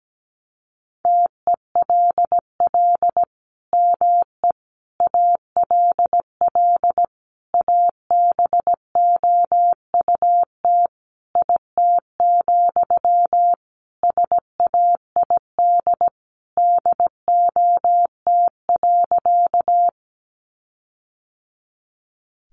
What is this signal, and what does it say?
Morse code, heard as a keyed tone, about 17 wpm, 700 Hz. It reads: TELL ME ALL ABOUT IT, SAID DOT.
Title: Sentences from Top 1500 Words 17wpm